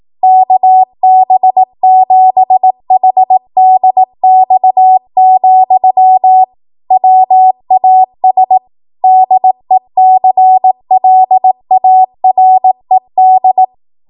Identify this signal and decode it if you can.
Morse code, heard as a keyed tone, about 18 wpm, 750 Hz. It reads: KB7HDX, WAS DECLARED